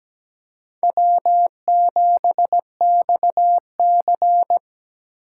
Morse code, heard as a keyed tone, about 20 wpm, 700 Hz. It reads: W7XC
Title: Speed Racing - USA 1x2 Call Signs 17wpm